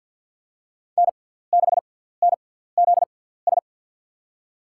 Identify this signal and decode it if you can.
Morse code, heard as a keyed tone, about 47 wpm, spaced as for 17 wpm, 700 Hz. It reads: N6NZS
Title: USA Call Signs - 6x Character Spacing - 50wpm